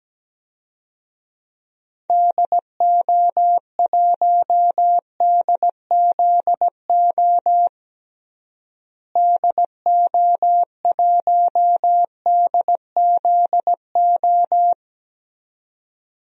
Morse code, heard as a keyed tone, about 17 wpm, 700 Hz. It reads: DO1DZO DO1DZO
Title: DXCC-100 Call Signs - 6 Characters - Repeated 2x - 17wpm